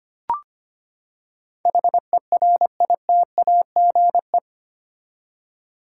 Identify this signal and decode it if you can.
Morse code, heard as a keyed tone, about 25 wpm, 700 Hz. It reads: HERITAGE
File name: Eight Letter Words - 25wpm